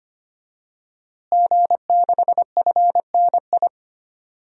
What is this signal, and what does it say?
Morse code, heard as a keyed tone, about 25 wpm, 700 Hz. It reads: G6FNI
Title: DXCC-100 Call Signs - 5 Characters - Repeated 3x - 25wpm